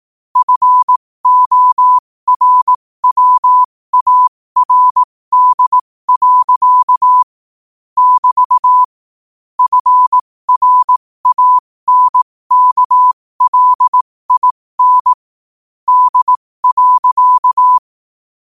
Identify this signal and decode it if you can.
Morse code, heard as a keyed tone, about 18 wpm, spaced as for 15 wpm, 1000 Hz. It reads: FORWARD. <BT> FRANKLIN D.
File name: qotdfiles / 15wpm